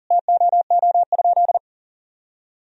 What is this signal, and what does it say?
Morse code, heard as a keyed tone, about 37 wpm, 700 Hz. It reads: TOO?
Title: Sentences from Top 1000 Words - Review Part 2 - 40wpm